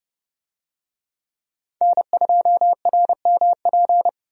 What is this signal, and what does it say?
Morse code, heard as a keyed tone, about 30 wpm, 700 Hz. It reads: N2RMP
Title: USA Call Signs - 5 Characters - 30wpm